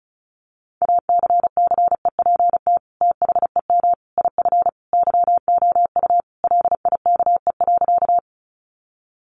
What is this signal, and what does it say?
Morse code, heard as a keyed tone, about 34 wpm, 700 Hz. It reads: ACCEPT THEM IF YOU LIKE.
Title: Sentences from Top 1300 Words 35wpm